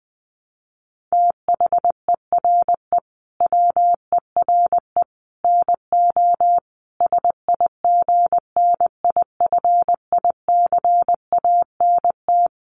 Morse code, heard as a keyed tone, about 19 wpm, 700 Hz. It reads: THERE WERE NO SIGNIFICANT
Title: Sets of 5 Words Encore2 20wpm